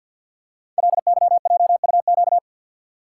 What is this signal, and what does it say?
Morse code, heard as a keyed tone, about 48 wpm, 700 Hz. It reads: LYJUQ
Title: Code Groups - 5 Letters - 50wpm